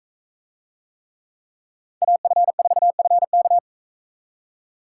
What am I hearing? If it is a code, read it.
Morse code, heard as a keyed tone, about 41 wpm, 700 Hz. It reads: AF4FK